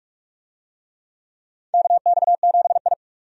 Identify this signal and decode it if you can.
Morse code, heard as a keyed tone, about 46 wpm, 700 Hz. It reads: KX7I
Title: USA Call Signs - 4 Characters - 45wpm